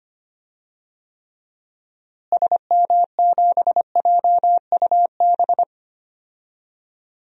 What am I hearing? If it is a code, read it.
Morse code, heard as a keyed tone, about 25 wpm, 700 Hz. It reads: SM7JUB
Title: DXCC-100 Call Signs - 6 Characters - Repeated 2x - 25wpm